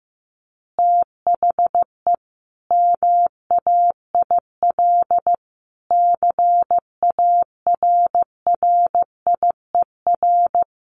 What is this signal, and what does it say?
Morse code, heard as a keyed tone, about 15 wpm, 700 Hz. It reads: THE MAIL CARRIER
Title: Additional Easy Phrases 15wpm